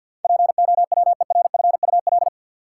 Morse code, heard as a keyed tone, about 48 wpm, 700 Hz. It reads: POWERFUL
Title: Sentences from Top 1300 Words - New Word Drill Alt - 50wpm